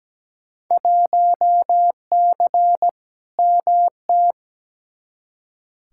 Morse code, heard as a keyed tone, about 17 wpm, 700 Hz. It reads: 1C MT
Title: Field Day Exchanges Encore - 17wpm